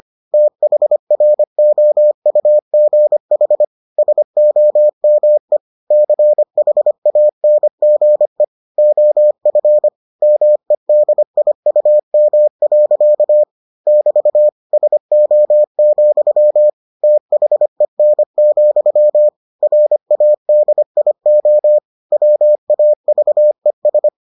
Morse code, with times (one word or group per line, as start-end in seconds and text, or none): 0.3-3.6 THROUGH
4.0-5.6 SOME
5.9-8.4 CHANGE
8.8-9.9 OF
10.2-13.4 MEDIUM.
13.9-14.5 <BT>
14.7-16.7 SO,
17.0-19.3 THEN,
19.6-21.8 RADIO
22.1-24.1 WAVES